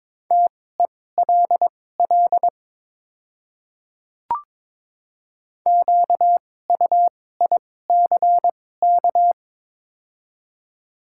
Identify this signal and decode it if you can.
Morse code, heard as a keyed tone, about 22 wpm, spaced as for 16 wpm, 700 Hz. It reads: TELL QUICK
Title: 2x Character Spacing - Top 500 Words - 22wpm